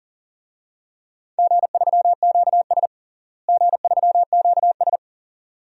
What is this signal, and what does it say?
Morse code, heard as a keyed tone, about 40 wpm, 700 Hz. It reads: G3QS G3QS
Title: DXCC-100 Call Signs - 4 Characters - Repeated 2x - 40wpm